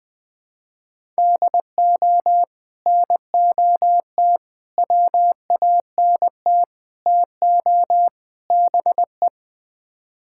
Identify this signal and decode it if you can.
Morse code, heard as a keyed tone, about 21 wpm, 700 Hz. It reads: DO NOT WANT TO BE
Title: Build A Phrase - 5-Word Phrases - 20wpm